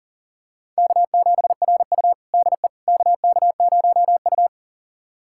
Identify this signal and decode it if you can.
Morse code, heard as a keyed tone, about 40 wpm, 700 Hz. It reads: K7RU DE KK0U
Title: USA Callsign DE Callsign - 4 Characters - 40wpm